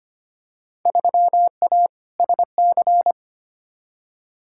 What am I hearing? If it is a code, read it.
Morse code, heard as a keyed tone, about 25 wpm, 700 Hz. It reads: 3A SC
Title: Field Day Exchanges Encore - 25wpm